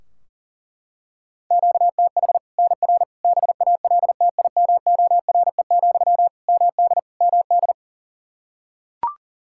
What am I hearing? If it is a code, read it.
Morse code, heard as a keyed tone, about 40 wpm, 700 Hz. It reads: QTH NR BALTIMORE, MD MD
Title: Practice QSO - Exchange 1 - Beginning - 40wpm